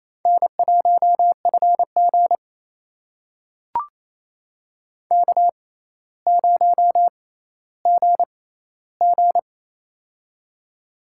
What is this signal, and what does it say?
Morse code, heard as a keyed tone, about 27 wpm, spaced as for 14 wpm, 700 Hz. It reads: N1FG K0GG
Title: USA Call Signs - 4 Characters - 6x Character Spacing - 28wpm